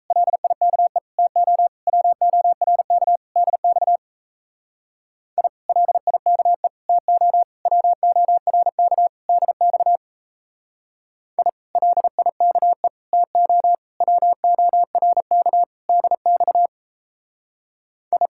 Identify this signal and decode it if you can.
Morse code, heard as a keyed tone, about 33 wpm, 700 Hz. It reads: LIKE TO WORK DX I LIKE TO WORK DX I LIKE TO WORK DX I
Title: Speed-Racing Common QSO Phrases - 28wpm